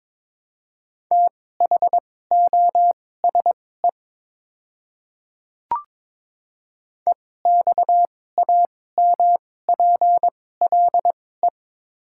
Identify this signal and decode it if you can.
Morse code, heard as a keyed tone, about 22 wpm, spaced as for 16 wpm, 700 Hz. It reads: THOSE EXAMPLE
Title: 2x Character Spacing - Top 300 Words - 22wpm